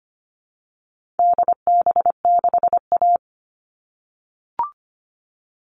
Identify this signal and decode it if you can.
Morse code, heard as a keyed tone, about 25 wpm, 700 Hz. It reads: DB6A
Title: All Cut Numbers - 4-Digit 25wpm